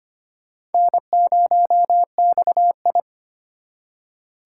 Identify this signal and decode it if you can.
Morse code, heard as a keyed tone, about 33 wpm, 700 Hz. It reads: N0XI